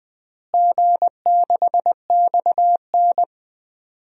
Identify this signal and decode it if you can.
Morse code, heard as a keyed tone, about 20 wpm, 700 Hz. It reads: G6XN
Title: DXCC-100 Call Signs - 4 Characters - Repeated 2x - 20wpm